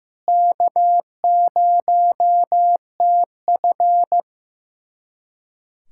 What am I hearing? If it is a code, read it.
Morse code, heard as a keyed tone, about 15 wpm, 700 Hz. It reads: K0TF
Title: USA Callsign DE Callsign - 4 Characters - 15wpm